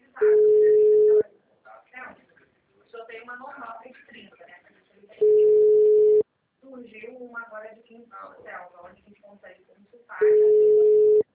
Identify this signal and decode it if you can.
Ringback tone (Europe, 425 Hz), with speech.